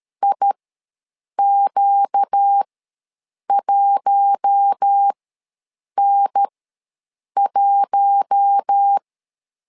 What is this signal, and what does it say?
Morse code, heard as a keyed tone, about 13 wpm, spaced as for 7 wpm, 800 Hz. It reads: IQ1N1
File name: LEZIONI / 4 Q1 ws 12wpm 2129